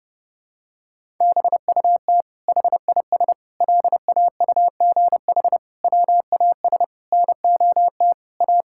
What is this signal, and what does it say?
Morse code, heard as a keyed tone, about 30 wpm, 700 Hz. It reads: BUT HIS LAUGH WAS NOT A
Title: Sentences from Top 1000 Words - Review Part 5 - 30wpm